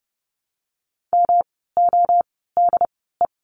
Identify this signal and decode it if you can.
Morse code, heard as a keyed tone, about 30 wpm, spaced as for 17 wpm, 700 Hz. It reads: MODE
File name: course / 30:10wpm / 108 Lesson 18f - Letter m - Words 30wpm